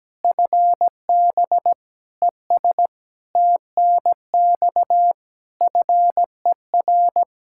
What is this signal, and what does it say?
Morse code, heard as a keyed tone, about 17 wpm, 700 Hz. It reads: FB ES TNX FER
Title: Practice QSO - Exchange 2 - Info - 17wpm